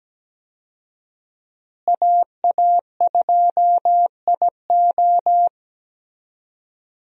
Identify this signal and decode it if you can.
Morse code, heard as a keyed tone, about 17 wpm, 700 Hz. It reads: AA2IO
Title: USA Call Signs - 5 Characters - Repeated 3x - 17wpm